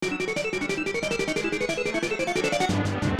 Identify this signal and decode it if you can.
There is no code.